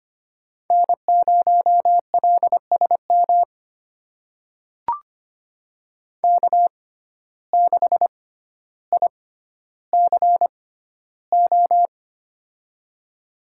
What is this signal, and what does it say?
Morse code, heard as a keyed tone, about 25 wpm, spaced as for 13 wpm, 700 Hz. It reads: N0LSM K6ICO
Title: USA Call Signs - 6x Character Spacing - 25wpm